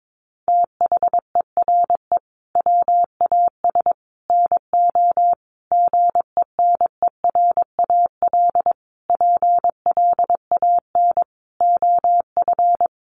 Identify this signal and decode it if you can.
Morse code, heard as a keyed tone, about 22 wpm, 700 Hz. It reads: THERE WAS NO GENERAL PLAN OF
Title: Sentences from Top 1200 Words 22wpm